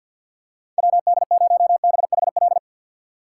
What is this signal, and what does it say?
Morse code, heard as a keyed tone, about 52 wpm, 700 Hz. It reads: WD0BHL
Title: USA Call Signs - 6x Character Spacing - 50wpm